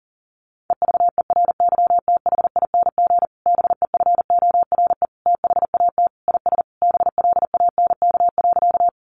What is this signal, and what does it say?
Morse code, heard as a keyed tone, about 40 wpm, 700 Hz. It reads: EVERYTHING BEFORE THAT IS BLANK.
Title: Sentences from Top 1500 Words 40wpm